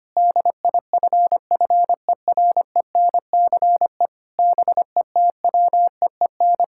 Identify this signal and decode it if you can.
Morse code, heard as a keyed tone, about 25 wpm, 700 Hz. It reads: DIFFERENCE BETWEEN
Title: Sentences from Top 1000 Words - Review Part 5 - 25wpm